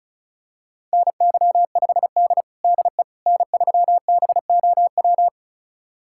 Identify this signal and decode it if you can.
Morse code, heard as a keyed tone, about 36 wpm, 700 Hz. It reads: NY5D DE N3BOW
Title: USA Callsign DE Callsign 35wpm